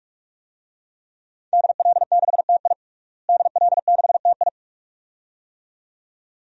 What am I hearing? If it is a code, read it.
Morse code, heard as a keyed tone, about 44 wpm, 700 Hz. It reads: DL6TI DL6TI
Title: DXCC-100 Call Signs - 5 Characters - Repeated 2x - 45wpm